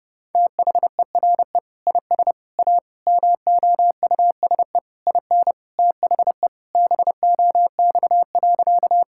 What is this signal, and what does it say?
Morse code, heard as a keyed tone, about 30 wpm, 700 Hz. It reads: THERE IS A MOUSE IN THE BOX.
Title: Sentences from Top 1000 Words - Review Part 5 - 30wpm